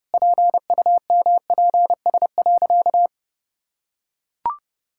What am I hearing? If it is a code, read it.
Morse code, heard as a keyed tone, about 30 wpm, 700 Hz. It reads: PUMPS.
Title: Pangram Challenge v2 - 30wpm